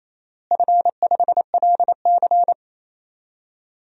Morse code, heard as a keyed tone, about 28 wpm, 700 Hz. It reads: F5LC
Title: DXCC-100 Call Signs - 4 Characters - Repeated 3x - 28wpm